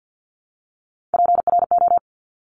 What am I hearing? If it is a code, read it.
Morse code, heard as a keyed tone, about 50 wpm, 700 Hz. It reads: PRO
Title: Top Three Letter Combinations in Words 50wpm